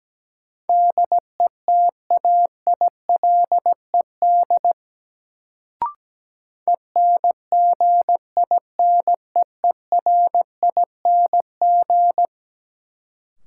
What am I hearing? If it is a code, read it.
Morse code, heard as a keyed tone, about 17 wpm, 700 Hz. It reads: DETAILED ENGINEERING
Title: Sentences from Top 1500 Words - New Word Drill Alt - 17wpm